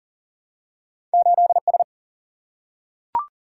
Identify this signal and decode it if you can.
Morse code, heard as a keyed tone, about 40 wpm, 700 Hz. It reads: 8 S
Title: Single-Letter Single-Number 40wpm